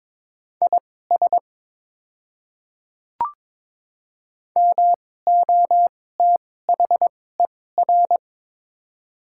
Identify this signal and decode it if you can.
Morse code, heard as a keyed tone, about 23 wpm, spaced as for 16 wpm, 700 Hz. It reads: IS MOTHER
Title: 2x Character Spacing - Top 200 Words - 22wpm